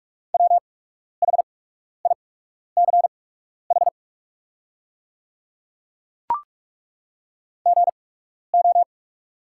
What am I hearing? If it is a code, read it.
Morse code, heard as a keyed tone, about 43 wpm, spaced as for 12 wpm, 700 Hz. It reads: WHICH GO
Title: 8x Character Spacing - Top 100 Words - 45wpm